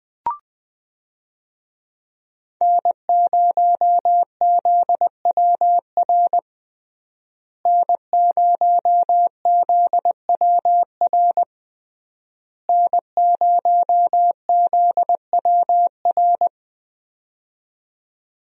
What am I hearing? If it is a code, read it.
Morse code, heard as a keyed tone, about 20 wpm, 700 Hz. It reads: N0ZWR N0ZWR N0ZWR
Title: USA Call Signs - 5 Characters - Repeated 3x - 20wpm